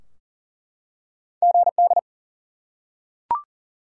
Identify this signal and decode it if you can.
Morse code, heard as a keyed tone, about 40 wpm, 700 Hz. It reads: GD
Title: Top 90 QSO Elements 40wpm